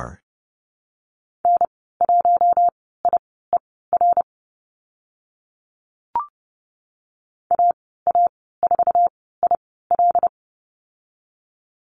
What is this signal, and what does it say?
Morse code, heard as a keyed tone, about 30 wpm, spaced as for 17 wpm, 700 Hz. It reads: N1IER AA4IL
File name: Lesson 14g - Number 2 - Call Signs 30wpm